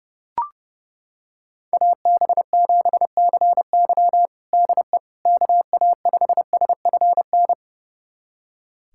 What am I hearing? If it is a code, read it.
Morse code, heard as a keyed tone, about 30 wpm, 700 Hz. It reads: AB7CY DE KA5SFN
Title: USA Callsign DE Callsign 30wpm